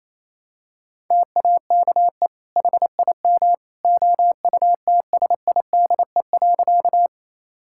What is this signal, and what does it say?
Morse code, heard as a keyed tone, about 28 wpm, 700 Hz. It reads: TAKE HIM OUTSIDE.
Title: Sentences from Top 1000 Words - Review Part 6 - 28wpm